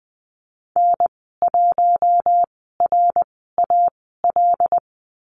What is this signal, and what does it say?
Morse code, heard as a keyed tone, about 20 wpm, spaced as for 15 wpm, 700 Hz. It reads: N1RAL
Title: 076 Lesson 13g - Letter L - Call Signs 20wpm